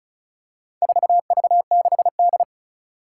Morse code, heard as a keyed tone, about 36 wpm, 700 Hz. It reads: 4V6D
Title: Four Challenging Letter-Number Combinations 35wpm